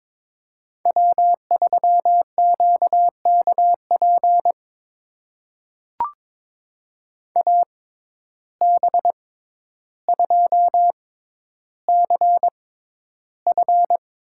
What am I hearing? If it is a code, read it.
Morse code, heard as a keyed tone, about 22 wpm, spaced as for 11 wpm, 700 Hz. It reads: W3QKP AB2CF
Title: USA Call Signs - 5 Characters - 6x Character Spacing - 22wpm